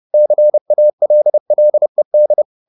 Morse code, heard as a keyed tone, about 30 wpm, 600 Hz. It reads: CALLED